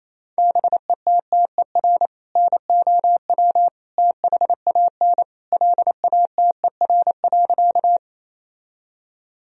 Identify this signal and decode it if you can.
Morse code, heard as a keyed tone, about 28 wpm, 700 Hz. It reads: BETTER NOW THAN LATER.
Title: Sentences from Top 1000 Words - Review Part 4 - 28wpm